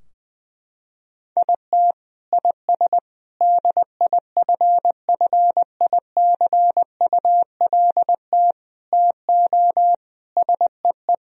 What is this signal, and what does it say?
Morse code, heard as a keyed tone, about 20 wpm, 700 Hz. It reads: IT IS DIFFICULT TO SEE
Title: Sentences from Top 1000 Words - Review Part 4 - 20wpm